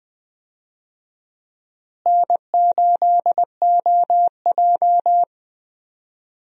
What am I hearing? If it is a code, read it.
Morse code, heard as a keyed tone, about 20 wpm, 700 Hz. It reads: N8OJ
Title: USA Call Signs - 4 Characters - Repeated 2x - 20wpm